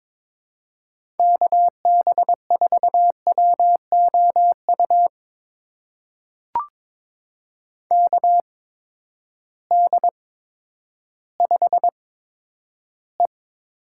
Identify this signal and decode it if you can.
Morse code, heard as a keyed tone, about 22 wpm, spaced as for 11 wpm, 700 Hz. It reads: KB4WOU KD5E
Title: USA Call Signs - 6 Characters - 8x Character Spacing - 22wpm